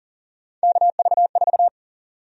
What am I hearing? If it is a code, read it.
Morse code, heard as a keyed tone, about 40 wpm, 700 Hz. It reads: KV4
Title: USA Call Sign Prefixes - 40wpm